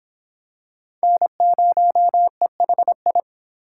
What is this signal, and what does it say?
Morse code, heard as a keyed tone, about 26 wpm, 700 Hz. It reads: N0EHI